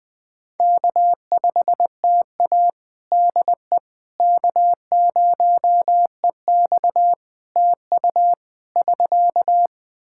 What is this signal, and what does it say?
Morse code, heard as a keyed tone, about 20 wpm, 700 Hz. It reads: K5TA DE K0EX TU <SK>
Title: Practice QSO - Exchange 4 - Ending - 20wpm